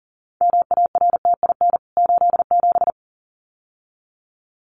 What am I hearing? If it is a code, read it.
Morse code, heard as a keyed tone, about 40 wpm, 700 Hz. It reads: MARTIN 87
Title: CWT Exchange 40wpm